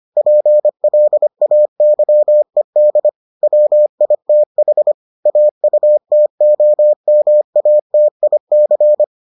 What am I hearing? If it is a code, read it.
Morse code, heard as a keyed tone, about 25 wpm, 600 Hz. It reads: PLAYED WITH AUTOMATIC